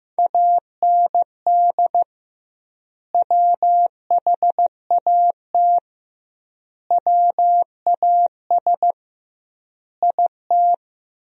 Morse code, heard as a keyed tone, about 16 wpm, 700 Hz. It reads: AND WHAT WAS IT
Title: 2x Word-Spacing - Sentences from Top 100 Words - 15wpm